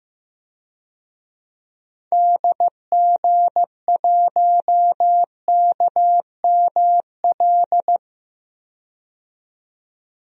Morse code, heard as a keyed tone, about 15 wpm, 700 Hz. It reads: DG1KML
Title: DXCC-100 Call Signs - 6 Characters - 15wpm